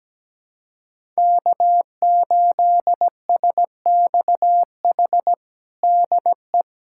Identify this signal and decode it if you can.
Morse code, heard as a keyed tone, about 17 wpm, 700 Hz. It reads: K8SXH DE
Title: USA Callsign DE Callsign 17wpm